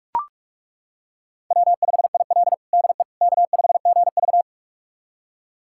Morse code, heard as a keyed tone, about 42 wpm, 700 Hz. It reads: W5IL DE K5GV
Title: USA Callsign DE Callsign - 4 Characters - 45wpm